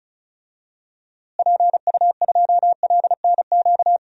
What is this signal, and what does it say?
Morse code, heard as a keyed tone, about 36 wpm, 700 Hz. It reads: PU2LNQ
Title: DXCC-100 Call Signs - 6 Characters - Repeated 2x - 35wpm